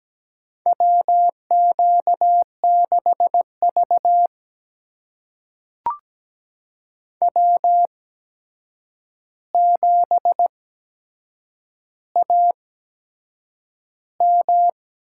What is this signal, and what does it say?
Morse code, heard as a keyed tone, about 17 wpm, spaced as for 7 wpm, 700 Hz. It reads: WQ6V W7AM
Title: USA Call Signs - 4 Characters - 8x Character Spacing - 17wpm